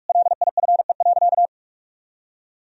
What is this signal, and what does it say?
Morse code, heard as a keyed tone, about 46 wpm, 700 Hz. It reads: LIFE.
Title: Sentences from Top 1000 Words - Review Part 3 - 45wpm